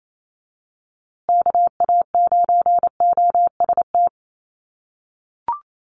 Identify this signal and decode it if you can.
Morse code, heard as a keyed tone, about 28 wpm, 700 Hz. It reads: KA9OST
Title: USA Call Signs Repeated 2x 28wpm